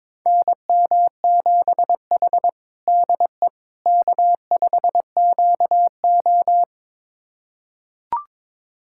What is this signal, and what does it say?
Morse code, heard as a keyed tone, about 22 wpm, 700 Hz. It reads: NM7H DE K5QO
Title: USA Callsign DE Callsign - 4 Characters - 22wpm